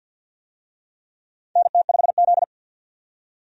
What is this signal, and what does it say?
Morse code, heard as a keyed tone, about 47 wpm, 700 Hz. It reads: NT5Z
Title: USA Call Signs - 4 Characters - 50wpm